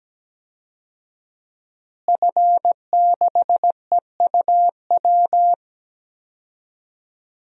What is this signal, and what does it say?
Morse code, heard as a keyed tone, about 17 wpm, 700 Hz. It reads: F6EUW
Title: DXCC-100 Call Signs - 5 Characters - Repeated 2x - 17wpm